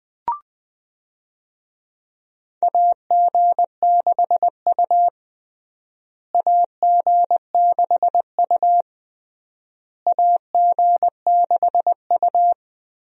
Morse code, heard as a keyed tone, about 20 wpm, 700 Hz. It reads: AG6U AG6U AG6U
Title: USA Call Signs - 4 Characters - Repeated 3x - 20wpm